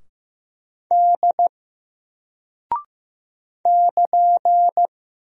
Morse code, heard as a keyed tone, about 15 wpm, 700 Hz. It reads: D <KN>
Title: Single Character (Letter:Number:Punctuation:Prosign) 15wpm